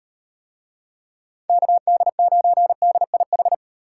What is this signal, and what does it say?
Morse code, heard as a keyed tone, about 35 wpm, 700 Hz. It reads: KD9DIH